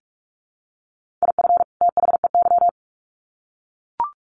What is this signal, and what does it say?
Morse code, heard as a keyed tone, about 44 wpm, 700 Hz. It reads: IF THEY
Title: Sets of 2 Words 45wpm